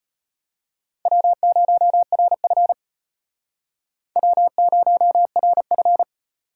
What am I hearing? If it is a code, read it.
Morse code, heard as a keyed tone, about 33 wpm, 700 Hz. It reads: W0RF W0RF